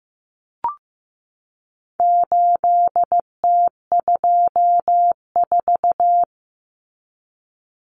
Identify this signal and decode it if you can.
Morse code, heard as a keyed tone, about 15 wpm, 700 Hz. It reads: 8T24
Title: Common Cut Numbers - 4-Digit 15wpm